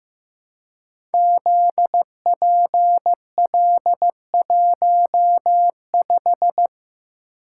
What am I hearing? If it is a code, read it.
Morse code, heard as a keyed tone, about 15 wpm, 700 Hz. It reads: ZPL15
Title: Code Groups - 5 Letter:Numbers - 15wpm